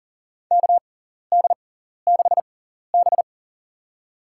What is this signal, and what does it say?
Morse code, heard as a keyed tone, about 40 wpm, spaced as for 14 wpm, 700 Hz. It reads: KD6B
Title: USA Call Signs - 4 Characters - 6x Character Spacing - 40wpm